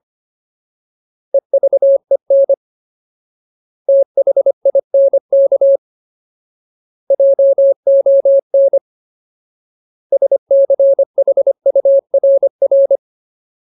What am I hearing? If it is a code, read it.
Morse code, heard as a keyed tone, about 25 wpm, 550 Hz. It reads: EVEN THINK JON SCHURR
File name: Ultimate Word List / 25WPM Ultimate